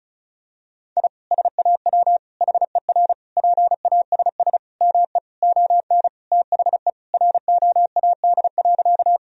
Morse code, heard as a keyed tone, about 34 wpm, 700 Hz. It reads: I SAW HER PASS ME ON THE ROAD.
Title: Sentences from Top 1000 Words - Review Part 2 - 35wpm